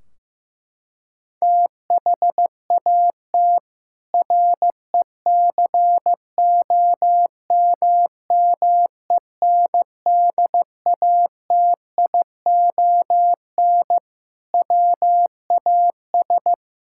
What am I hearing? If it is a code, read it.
Morse code, heard as a keyed tone, about 15 wpm, 700 Hz. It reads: THAT RECOMMENDATION WAS
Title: Sentences from Top 1600 Words 15wpm